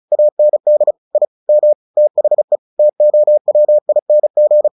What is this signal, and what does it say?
Morse code, heard as a keyed tone, about 34 wpm, 600 Hz. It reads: AND I M THE TOWING
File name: the old man and the sea / the old man and the sea 1 35wpm